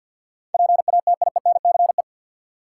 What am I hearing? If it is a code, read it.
Morse code, heard as a keyed tone, about 52 wpm, 700 Hz. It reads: PATIENCE
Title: Sentences from Top 1300 Words - New Word Drill Alt - 50wpm